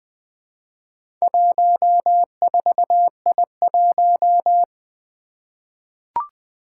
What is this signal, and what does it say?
Morse code, heard as a keyed tone, about 20 wpm, 700 Hz. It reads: 14I1
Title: Four Challenging Letter-Number Combinations 20wpm